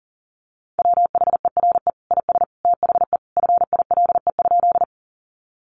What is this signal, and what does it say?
Morse code, heard as a keyed tone, about 40 wpm, 700 Hz. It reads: WHERE IS THE FILE?
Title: Sentences from Top 1200 Words 40wpm